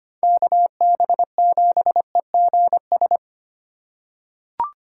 Morse code, heard as a keyed tone, about 25 wpm, 700 Hz. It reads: KB7EGS
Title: USA Call Signs - 6 Characters - 8x Character Spacing - 25wpm